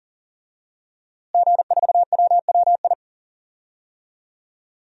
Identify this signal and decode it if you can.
Morse code, heard as a keyed tone, about 39 wpm, 700 Hz. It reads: G4WWI
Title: DXCC-100 Call Signs - 5 Characters - Repeated 3x - 40wpm